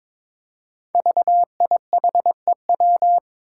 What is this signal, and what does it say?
Morse code, heard as a keyed tone, about 22 wpm, 700 Hz. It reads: VIHEW
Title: Code Groups - 5 Letter:Numbers - 22wpm